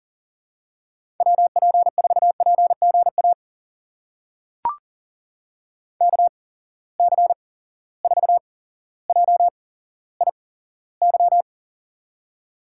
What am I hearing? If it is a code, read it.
Morse code, heard as a keyed tone, about 39 wpm, spaced as for 17 wpm, 700 Hz. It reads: WP4PGA KC4JIY
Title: USA Call Signs - 6 Characters - 8x Character Spacing - 40wpm